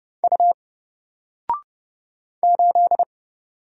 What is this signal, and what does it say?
Morse code, heard as a keyed tone, about 30 wpm, 700 Hz. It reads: U 8
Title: Single Character (Letter:Number:Punctuation:Prosign) 30wpm